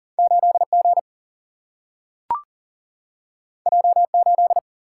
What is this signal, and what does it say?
Morse code, heard as a keyed tone, about 40 wpm, 700 Hz. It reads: 8 G J 8
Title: Single-Letter Single-Number 40wpm